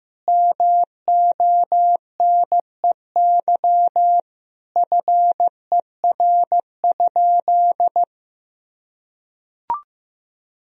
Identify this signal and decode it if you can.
Morse code, heard as a keyed tone, about 15 wpm, 700 Hz. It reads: MONEY FER?
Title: Sentences from Top 400 Words and Abbreviations 15wpm